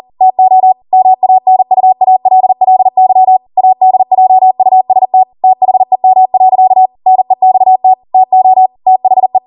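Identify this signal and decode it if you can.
Morse code, heard as a keyed tone, about 40 wpm, 750 Hz. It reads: TO MANUALLY ADJUST THEM. NEXT TO THE
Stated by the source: Code practice at 40 wpm.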